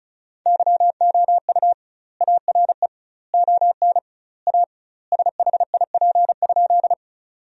Morse code, heard as a keyed tone, about 36 wpm, 700 Hz. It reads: YOU ARE ON A SHIP?
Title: 2x Word-Spacing - Sentences from Top 400 Words - 35wpm